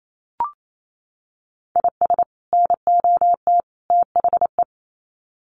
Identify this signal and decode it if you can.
Morse code, heard as a keyed tone, about 28 wpm, 700 Hz. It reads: IS NOT THE
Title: Sets of 3 Words 28wpm